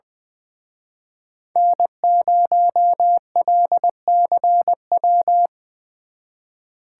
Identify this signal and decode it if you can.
Morse code, heard as a keyed tone, about 22 wpm, 700 Hz. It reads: N0LCW
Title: Speed Racing - USA Call Signs 20wpm